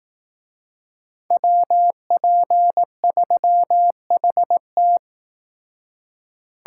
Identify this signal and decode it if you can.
Morse code, heard as a keyed tone, about 18 wpm, 700 Hz. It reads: WP3HT
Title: Speed Racing - USA 2x2 Call Signs 15wpm